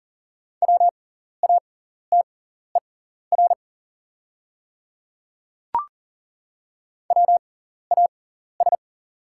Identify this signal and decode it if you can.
Morse code, heard as a keyed tone, about 40 wpm, spaced as for 14 wpm, 690 Hz. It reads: WATER WAS
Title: 6x Character Spacing - Top 200 Words - 40wpm